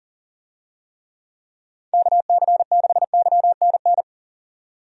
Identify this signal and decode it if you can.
Morse code, heard as a keyed tone, about 40 wpm, 700 Hz. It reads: KC6YNN
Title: USA Call Signs - 6 Characters - 40wpm